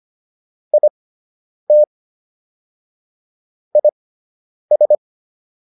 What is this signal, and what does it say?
Morse code, heard as a keyed tone, about 25 wpm, spaced as for 9 wpm, 600 Hz. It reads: IT IS